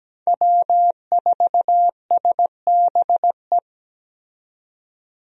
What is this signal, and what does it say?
Morse code, heard as a keyed tone, about 17 wpm, 700 Hz. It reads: W4SBE
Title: USA Call Signs - 5 Characters - 17wpm